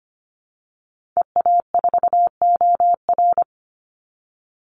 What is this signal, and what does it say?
Morse code, heard as a keyed tone, about 25 wpm, 700 Hz. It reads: EA4OR
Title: Call Signs Repeated 2x - 25wpm